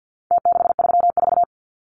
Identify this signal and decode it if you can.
Morse code, heard as a keyed tone, about 50 wpm, 700 Hz. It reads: T634
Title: Numbers - Contest Style - 4-Digit 50wpm